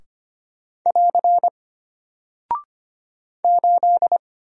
Morse code, heard as a keyed tone, about 25 wpm, 700 Hz. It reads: <AR> 8
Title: Single Character (Letter:Number:Punctuation:Prosign) Alt 25wpm